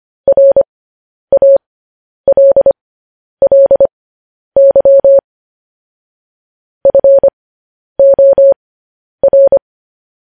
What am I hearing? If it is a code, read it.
Morse code, heard as a keyed tone, about 25 wpm, spaced as for 10 wpm, 550 Hz. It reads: RALLY FOR
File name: podcast / News headlines-1x-25x10